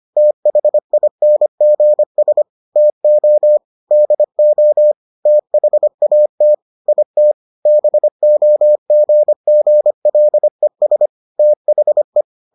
Morse code, heard as a keyed tone, about 25 wpm, 600 Hz. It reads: THINGS TO DO THAT IT BOGGLES THE